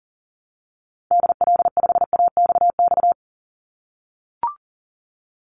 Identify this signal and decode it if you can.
Morse code, heard as a keyed tone, about 40 wpm, 700 Hz. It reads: DL5AXX
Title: Call Signs - 40wpm